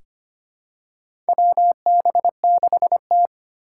Morse code, heard as a keyed tone, about 25 wpm, 700 Hz. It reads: WB6T
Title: USA Call Signs - 4 Characters - 3x Character Spacing - 25wpm